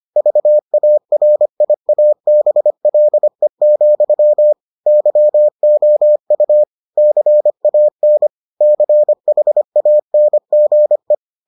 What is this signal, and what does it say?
Morse code, heard as a keyed tone, about 25 wpm, 600 Hz. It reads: VARIABLE, YOU CAN CHANGE